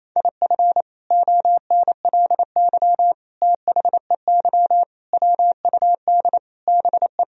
Morse code, heard as a keyed tone, about 28 wpm, 700 Hz. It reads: IF ONLY THEY WUD BE
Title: Sentences from Top 500 Words and Abbreviations 28wpm